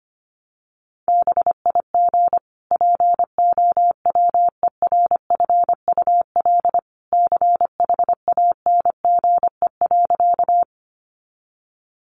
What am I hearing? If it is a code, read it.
Morse code, heard as a keyed tone, about 25 wpm, 700 Hz. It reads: BIG POWERFUL CHANGE.
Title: Sentences from Top 1300 Words 25wpm